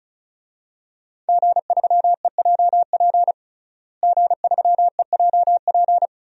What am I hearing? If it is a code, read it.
Morse code, heard as a keyed tone, about 34 wpm, 700 Hz. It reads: G3EJP G3EJP
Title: DXCC-100 Call Signs - 5 Characters - Repeated 3x - 35wpm